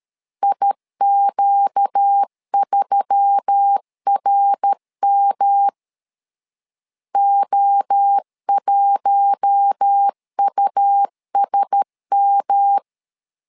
Morse code, heard as a keyed tone, about 13 wpm, 800 Hz. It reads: IQ3RM O1USM
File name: LEZIONI / 4 Q1S3 12wpm 2023